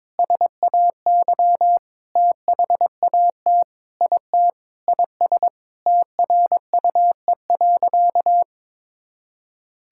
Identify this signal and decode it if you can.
Morse code, heard as a keyed tone, about 22 wpm, 700 Hz. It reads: SAY THAT IT IS TRUE.
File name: practice / 22wpm / Sentences with 3-Word Phrases - 22wpm